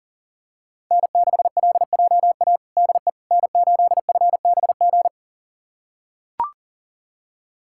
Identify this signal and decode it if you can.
Morse code, heard as a keyed tone, about 40 wpm, 700 Hz. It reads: N6LJA DE N8FBG
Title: USA Callsign DE Callsign - 5 Characters - 40wpm